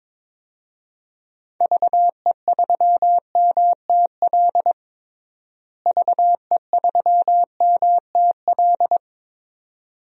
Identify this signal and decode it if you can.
Morse code, heard as a keyed tone, about 23 wpm, 700 Hz. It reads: VE3MTL VE3MTL
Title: DXCC-100 Call Signs - 6 Characters - Repeated 2x - 22wpm